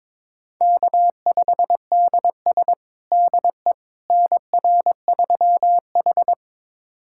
Morse code, heard as a keyed tone, about 22 wpm, 700 Hz. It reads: K5DS DE NR3H
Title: USA Callsign DE Callsign - 4 Characters - 22wpm